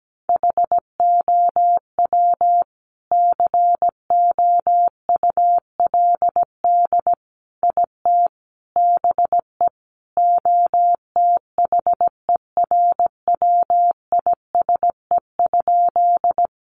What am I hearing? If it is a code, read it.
Morse code, heard as a keyed tone, about 17 wpm, 700 Hz. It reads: HOW COULD IT BE OTHERWISE?
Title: Sentences from Top 1200 Words 17wpm